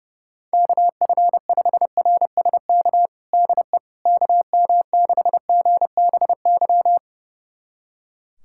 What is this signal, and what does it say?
Morse code, heard as a keyed tone, about 30 wpm, 700 Hz. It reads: KF5RSK DE KM6GBY
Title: USA Callsign DE Callsign - 6 Characters - 30wpm